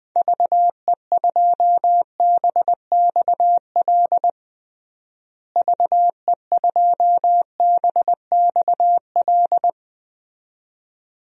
Morse code, heard as a keyed tone, about 20 wpm, 700 Hz. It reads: VE2BXL VE2BXL
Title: DXCC-100 Call Signs - 6 Characters - Repeated 2x - 20wpm